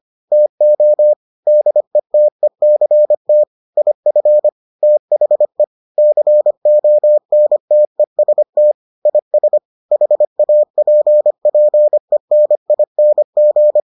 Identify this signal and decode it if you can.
Morse code, heard as a keyed tone, about 25 wpm, 600 Hz. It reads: TO DETECT IF THE CONTEST IS HAPPENING